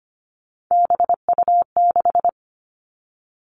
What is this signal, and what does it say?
Morse code, heard as a keyed tone, about 25 wpm, 700 Hz. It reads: BU6
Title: Three Challenging Letter-Number Combinations 25wpm